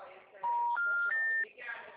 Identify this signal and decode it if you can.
Special information tone, with speech.
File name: out- -2017- - - .56519